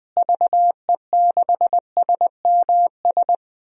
Morse code, heard as a keyed tone, about 21 wpm, 700 Hz. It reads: VE6SMS
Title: DXCC-100 Call Signs - 6 Characters - Repeated 2x - 20wpm